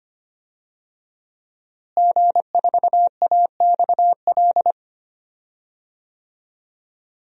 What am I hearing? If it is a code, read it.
Morse code, heard as a keyed tone, about 25 wpm, 700 Hz. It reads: G4AXL
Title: DXCC-100 Call Signs - 5 Characters - Repeated 3x - 25wpm